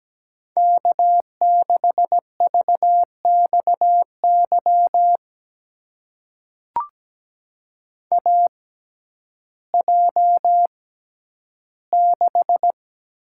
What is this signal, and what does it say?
Morse code, heard as a keyed tone, about 17 wpm, spaced as for 10 wpm, 700 Hz. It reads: K6VXY AJ6
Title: USA Call Signs - 6x Character Spacing - 17wpm